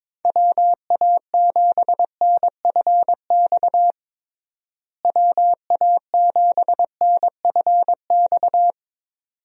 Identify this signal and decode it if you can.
Morse code, heard as a keyed tone, about 23 wpm, 700 Hz. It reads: WA7NFX WA7NFX
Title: USA Call Signs - 6 Characters - Repeated 2x - 22wpm